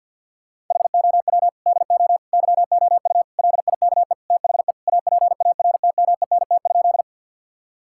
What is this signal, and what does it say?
Morse code, heard as a keyed tone, about 50 wpm, 700 Hz. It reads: HOW DO YOU LIKE THE APARTMENT?
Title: Sentences from Top 1000 Words - Review Part 6 - 50wpm